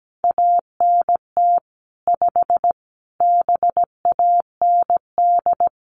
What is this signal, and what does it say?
Morse code, heard as a keyed tone, about 17 wpm, 700 Hz. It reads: ANT 5 BAND
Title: Common QSO Phrases 17wpm